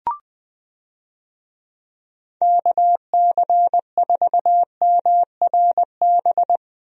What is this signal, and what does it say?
Morse code, heard as a keyed tone, about 20 wpm, 700 Hz. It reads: KC4MRB
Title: USA Call Signs - 6 Characters - Repeated 2x - 20wpm